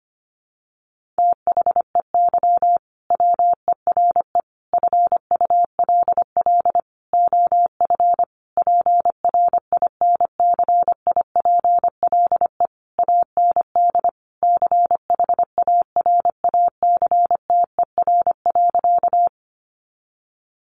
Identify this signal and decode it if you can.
Morse code, heard as a keyed tone, about 25 wpm, 700 Hz. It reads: THEY WERE FULL OF PRINCIPLE AND CHARACTER.
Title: Sentences from Top 1400 Words 25wpm